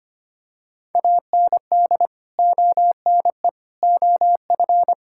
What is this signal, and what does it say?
Morse code, heard as a keyed tone, about 25 wpm, 700 Hz. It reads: AND ONE OF
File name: practice / 25wpm / Sentences with 4-Word Phrases - 25wpm